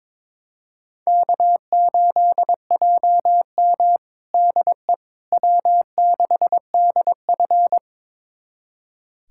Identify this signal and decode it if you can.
Morse code, heard as a keyed tone, about 22 wpm, 700 Hz. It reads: K8JM DE W6DF
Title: USA Callsign DE Callsign - 4 Characters - 22wpm